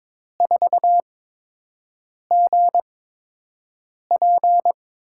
Morse code, heard as a keyed tone, about 22 wpm, spaced as for 6 wpm, 700 Hz. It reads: 4GP
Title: USA Call Signs - 6 Characters - 8x Character Spacing - 22wpm